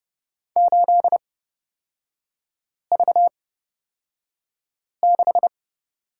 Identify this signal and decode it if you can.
Morse code, heard as a keyed tone, about 31 wpm, spaced as for 5 wpm, 700 Hz. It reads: 8V6
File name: Lesson 38e - Letter q - Three Characters 30wpm